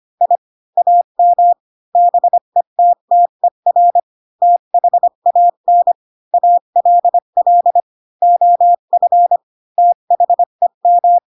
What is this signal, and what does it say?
Morse code, heard as a keyed tone, about 25 wpm, 700 Hz. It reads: I AM BETTER THAN ALL OF THEM
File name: Pinocchio / Pinocchio ch8 20 25 ik1hge